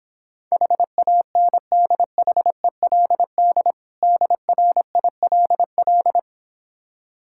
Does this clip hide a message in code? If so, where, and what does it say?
Yes: Morse code from 0.5 to 6.2 s, heard as a keyed tone, about 26 wpm, 700 Hz. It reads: HANDHELD DRILL